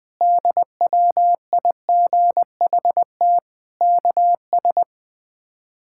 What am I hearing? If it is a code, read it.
Morse code, heard as a keyed tone, about 21 wpm, 700 Hz. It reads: DWIGHT KS
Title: K1USN SST QSO Practice - 20wpm